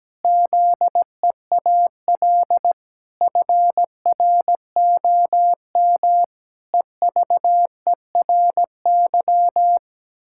Morse code, heard as a keyed tone, about 17 wpm, 700 Hz. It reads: ZEAL FROM EVERY
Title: Pangram Challenge v2 - 17wpm